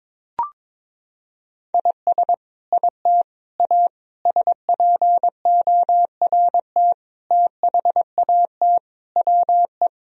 Morse code, heard as a keyed tone, about 21 wpm, 700 Hz. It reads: IS IT A SPORT THAT WE
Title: Sentences from Top 1000 Words - Review Part 4 - 22wpm